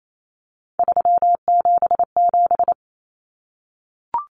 Morse code, heard as a keyed tone, about 28 wpm, 700 Hz. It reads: 377
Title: Common Cut Numbers - 3-Digit 28wpm